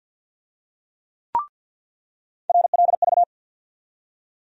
Morse code, heard as a keyed tone, about 46 wpm, 700 Hz. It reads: RLV
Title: USA Call Sign Postfixes - 3 Letters 50wpm